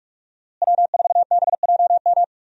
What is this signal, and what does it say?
Morse code, heard as a keyed tone, about 44 wpm, 700 Hz. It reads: W4BJM
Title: USA Call Signs - 5 Characters - Repeated 3x - 45wpm